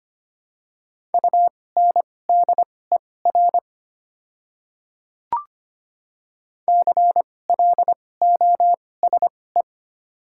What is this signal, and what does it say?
Morse code, heard as a keyed tone, about 25 wpm, spaced as for 18 wpm, 690 Hz. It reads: UNDER CLOSE
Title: 2x Character Spacing - Top 500 Words - 25wpm